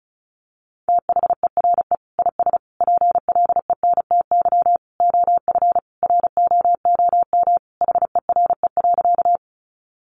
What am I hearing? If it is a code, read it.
Morse code, heard as a keyed tone, about 34 wpm, 700 Hz. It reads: THERE IS PLENTY OF ROOM HERE.
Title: Sentences from Top 1300 Words 35wpm